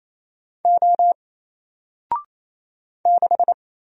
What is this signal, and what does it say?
Morse code, heard as a keyed tone, about 28 wpm, 700 Hz. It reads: O 6
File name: Single-Letter-Number-28wpm